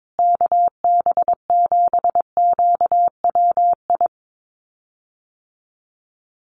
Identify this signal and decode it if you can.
Morse code, heard as a keyed tone, about 22 wpm, 700 Hz. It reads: KB7QWI
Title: USA Call Signs Repeated 3x 22wpm